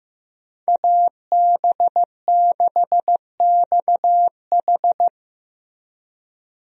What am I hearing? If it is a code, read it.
Morse code, heard as a keyed tone, about 15 wpm, 700 Hz. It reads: AB6XH
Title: USA Call Signs - 5 Characters - 15wpm